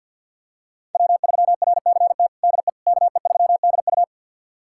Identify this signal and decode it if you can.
Morse code, heard as a keyed tone, about 52 wpm, 700 Hz. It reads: W3RCT DE KE3DU
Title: USA Callsign DE Callsign - 5 Characters - 50wpm